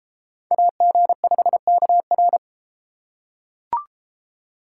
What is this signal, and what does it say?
Morse code, heard as a keyed tone, about 30 wpm, 700 Hz. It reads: AG5KR